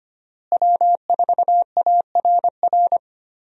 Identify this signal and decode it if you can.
Morse code, heard as a keyed tone, about 25 wpm, 700 Hz. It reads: W4ARR
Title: USA Call Signs - 5 Characters - Repeated 3x - 25wpm